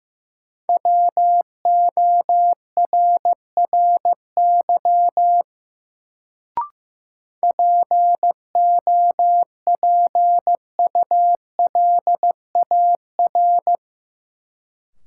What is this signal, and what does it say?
Morse code, heard as a keyed tone, about 15 wpm, 700 Hz. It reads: WORRY POPULAR
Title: Sets of 5 Words - New Word Drill - 15wpm